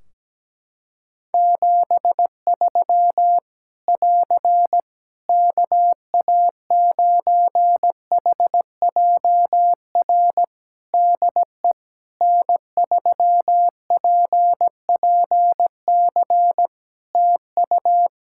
Morse code, heard as a keyed tone, about 17 wpm, 700 Hz. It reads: 73 <AR> KA9HJR DE N3PPC TU
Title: Practice QSO - Exchange 4 - Ending - 17wpm